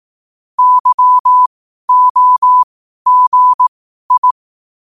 Morse code, heard as a keyed tone, about 18 wpm, spaced as for 12 wpm, 1000 Hz. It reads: YOGI
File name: qotdfiles / 12wpm